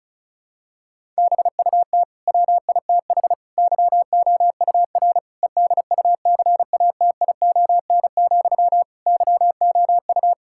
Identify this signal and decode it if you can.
Morse code, heard as a keyed tone, about 36 wpm, 700 Hz. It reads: BUT WITH YOUR EDUCATION, YOU
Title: Sentences from Top 1000 Words - Review Part 6 - 35wpm